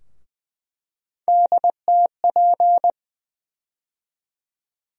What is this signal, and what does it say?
Morse code, heard as a keyed tone, about 20 wpm, 700 Hz. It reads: DTP
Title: USA Call Sign Postfixes - 3 Letters 20wpm